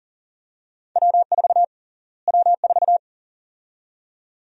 Morse code, heard as a keyed tone, about 40 wpm, 700 Hz. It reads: W4 W4
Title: USA Call Sign Prefixes - Repeated 2x - 40wpm